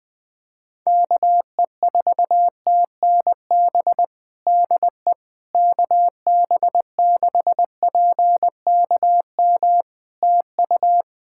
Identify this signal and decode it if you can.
Morse code, heard as a keyed tone, about 20 wpm, 700 Hz. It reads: KE4TNB DE KB6PKM TU
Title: Practice QSO - Exchange 4 - Ending - 20wpm